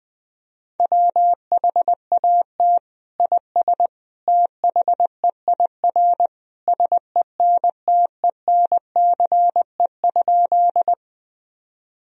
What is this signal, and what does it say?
Morse code, heard as a keyed tone, about 20 wpm, 700 Hz. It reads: WHAT IS THEIR SENTENCE?
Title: Sentences from Top 1000 Words - Review Part 1 - 20wpm